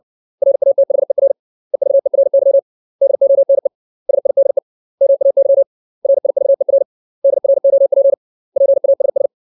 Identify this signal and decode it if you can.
Morse code, heard as a keyed tone, about 60 wpm, 550 Hz. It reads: RATHER EVERY DONE SIDE MAY RIVER DROP PASS
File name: 632 words / 632 Words 60WPM